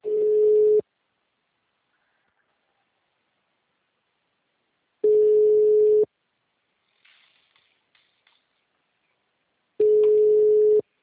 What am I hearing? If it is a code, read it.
Ringback tone (Europe, 425 Hz).
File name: out- -2011- - - .320